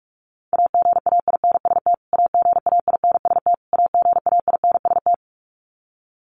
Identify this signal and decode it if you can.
Morse code, heard as a keyed tone, about 44 wpm, 700 Hz. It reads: AGAINST AGAINST AGAINST
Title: Top 500 Words Repeated 45wpm